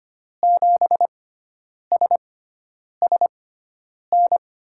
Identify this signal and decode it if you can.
Morse code, heard as a keyed tone, about 25 wpm, spaced as for 9 wpm, 700 Hz. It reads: 7SSN
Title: USA Call Signs - 6x Character Spacing - 25wpm